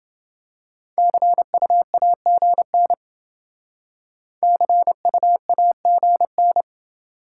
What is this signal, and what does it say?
Morse code, heard as a keyed tone, about 28 wpm, 700 Hz. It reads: CUAGN CUAGN